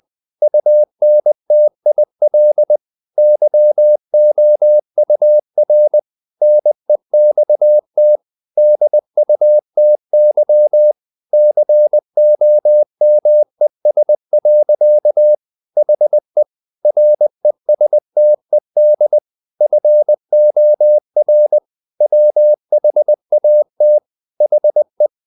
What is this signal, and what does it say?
Morse code, heard as a keyed tone, about 20 wpm, 600 Hz. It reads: UNTIL YOUR NEXT DUTY COMES. HE RESTED FOR WHAT HE
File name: the old man and the sea part 2 20wpm